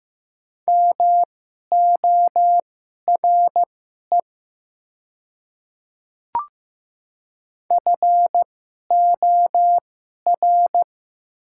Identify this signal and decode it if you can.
Morse code, heard as a keyed tone, about 15 wpm, spaced as for 11 wpm, 700 Hz. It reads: MORE FOR
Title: 2x Character Spacing - Top 100 Words - 15wpm